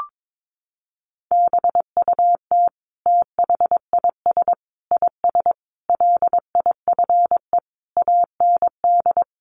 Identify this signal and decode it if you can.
Morse code, heard as a keyed tone, about 22 wpm, 700 Hz. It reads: BUT THIS IS LIFE AND
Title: Sentences from Top 1100 Words 22wpm